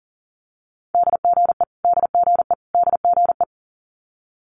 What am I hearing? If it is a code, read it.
Morse code, heard as a keyed tone, about 43 wpm, 700 Hz. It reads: DGE DGE DGE
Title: Multi-letter Phonemes Repeated 3x 40wpm